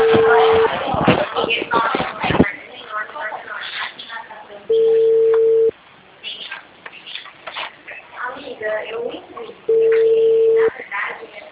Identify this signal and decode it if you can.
Ringback tone (China, 450 Hz), with speech.
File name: out- -2013- - - .39895